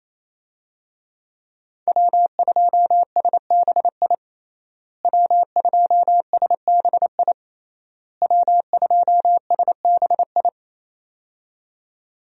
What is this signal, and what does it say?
Morse code, heard as a keyed tone, about 28 wpm, 700 Hz. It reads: W2SBI W2SBI W2SBI
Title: USA Call Signs - 5 Characters - Repeated 3x - 28wpm